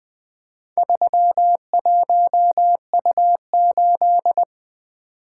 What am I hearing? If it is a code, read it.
Morse code, heard as a keyed tone, about 24 wpm, 700 Hz. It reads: 31U8
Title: Speed-Racing - Four Letter-Number Combinations - 20wpm